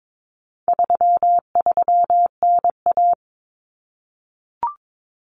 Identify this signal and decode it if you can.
Morse code, heard as a keyed tone, about 22 wpm, 700 Hz. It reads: 33NA
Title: Common Cut Numbers - 4-Digit 22wpm